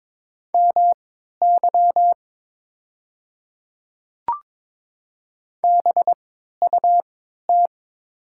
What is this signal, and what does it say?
Morse code, heard as a keyed tone, about 22 wpm, spaced as for 13 wpm, 700 Hz. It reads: MY BUT
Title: 3x Character Spacing - Top 100 Words - 22wpm